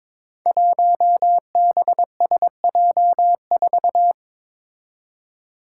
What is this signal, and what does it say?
Morse code, heard as a keyed tone, about 22 wpm, 700 Hz. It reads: 1BSJ4
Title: Five Challenging Letter-Number Combinations 22wpm